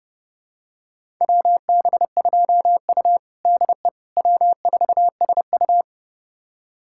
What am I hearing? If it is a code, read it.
Morse code, heard as a keyed tone, about 30 wpm, 700 Hz. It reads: WB2U DE W4SU
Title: USA Callsign DE Callsign - 4 Characters - 30wpm